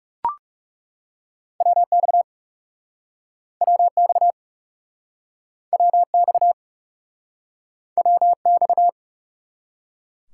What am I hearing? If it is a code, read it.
Morse code, heard as a keyed tone, about 40 wpm, 700 Hz. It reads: WX WX WX WX